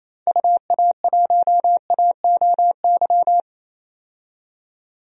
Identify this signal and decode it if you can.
Morse code, heard as a keyed tone, about 28 wpm, 700 Hz. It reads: UA1AOY
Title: DXCC-100 Call Signs - 6 Characters - 28wpm